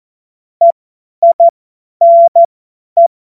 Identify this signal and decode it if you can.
Morse code, heard as a keyed tone, about 14 wpm, spaced as for 10 wpm, 680 Hz. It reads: EINE
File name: Witz-09-September-2025-50-F70